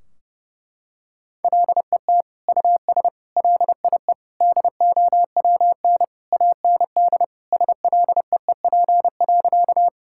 Morse code, heard as a keyed tone, about 32 wpm, 700 Hz. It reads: LET US LIE DOWN AND SLEEP.
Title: Sentences from Top 1600 Words 30wpm